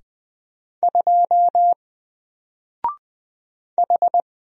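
Morse code, heard as a keyed tone, about 20 wpm, 700 Hz. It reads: 2 H
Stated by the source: Single letter and number Morse drill at 20 wpm, from N4LCD.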